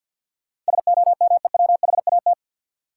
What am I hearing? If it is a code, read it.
Morse code, heard as a keyed tone, about 53 wpm, 700 Hz. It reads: SOMEWHAT
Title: Sentences from Top 1000 Words - Review Part 1 - New Word Drill - 50wpm